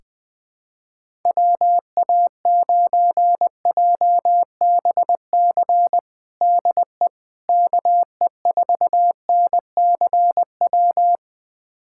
Morse code, heard as a keyed tone, about 20 wpm, 700 Hz. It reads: WA9JBC DE KE4NCW
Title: USA Callsign DE Callsign - 6 Characters - 20wpm